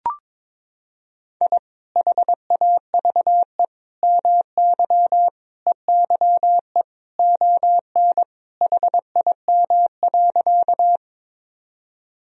Morse code, heard as a keyed tone, about 22 wpm, 700 Hz. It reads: I HAVE MY EYE ON HIM.
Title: Sentences from Top 1000 Words - Review Part 1 - 22wpm